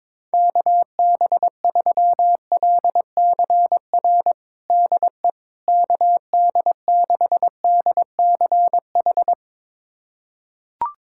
Morse code, heard as a keyed tone, about 22 wpm, 700 Hz. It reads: KB3LCR DE KD6DCH
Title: USA Callsign DE Callsign - 6 Characters - 22wpm